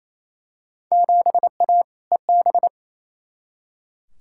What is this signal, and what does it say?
Morse code, heard as a keyed tone, about 28 wpm, 700 Hz. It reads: 7A EB
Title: Field Day Exchanges Encore - 28wpm